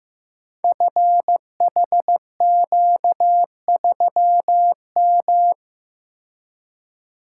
Morse code, heard as a keyed tone, about 15 wpm, 700 Hz. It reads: FHQ3M
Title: Code Groups - 5 Letter:Numbers - 15wpm